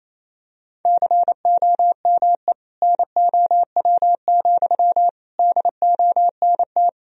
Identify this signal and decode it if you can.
Morse code, heard as a keyed tone, about 27 wpm, 700 Hz. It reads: COME NOW, DONT
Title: Sentences from Top 1000 Words - Review Part 2 - 28wpm